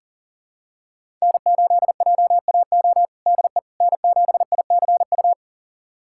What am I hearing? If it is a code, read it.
Morse code, heard as a keyed tone, about 37 wpm, 700 Hz. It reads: N8JAO DE N7ICU
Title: USA Callsign DE Callsign 40wpm